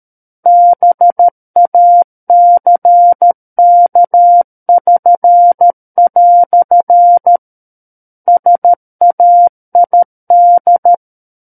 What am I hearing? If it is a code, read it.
Morse code, heard as a keyed tone, about 13 wpm, 700 Hz. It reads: BACK<SN>" SAID
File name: Book One / 04-Chapter 4 - 13 WPM